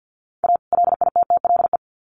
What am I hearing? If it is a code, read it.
Morse code, heard as a keyed tone, about 50 wpm, 700 Hz. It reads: A LITTLE
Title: Sets of 2 Words 50wpm